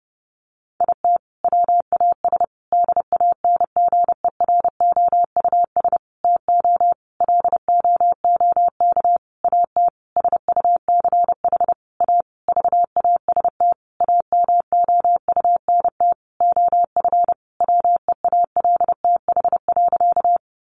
Morse code, heard as a keyed tone, about 30 wpm, 700 Hz. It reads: IT WAS DANGEROUS TO LOOK AT SUCH A VAST AMOUNT OF WEALTH.
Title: Sentences from Top 1300 Words 30wpm